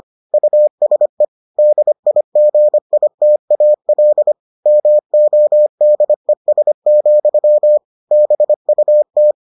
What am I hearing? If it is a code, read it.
Morse code, heard as a keyed tone, about 25 wpm, 600 Hz. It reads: USE DIGITAL MODES, BUT